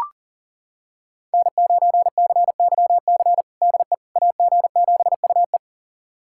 Morse code, heard as a keyed tone, about 40 wpm, 700 Hz. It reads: N9CYC DE AG7UE
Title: USA Callsign DE Callsign - 5 Characters - 40wpm